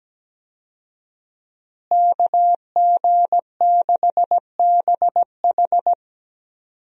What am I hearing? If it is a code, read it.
Morse code, heard as a keyed tone, about 17 wpm, 700 Hz. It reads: KG6BH
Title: USA Call Signs - 5 Characters - Repeated 2x - 17wpm